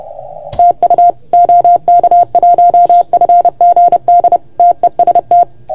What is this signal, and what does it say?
Morse code, heard as a keyed tone, about 30 wpm, 690 Hz. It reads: TU OK1FGD TEST
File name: Archiv / CUC381 / OK1FGD CUC 381